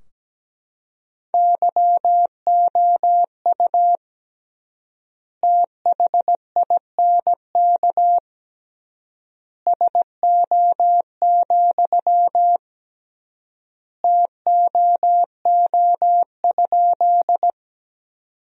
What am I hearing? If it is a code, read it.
Morse code, heard as a keyed tone, about 17 wpm, 700 Hz. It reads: YOU THINK SO, TOO?
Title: 3x Word-Spacing - Sentences from Top 200 Words - 17wpm